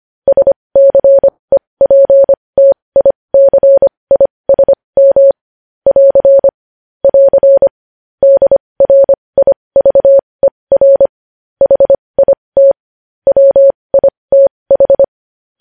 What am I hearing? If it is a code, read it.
Morse code, heard as a keyed tone, about 25 wpm, spaced as for 20 wpm, 550 Hz. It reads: SCEPTICISM <AR> <AR> DRIVER HIT WITH
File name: podcast / News headlines-1x-25x20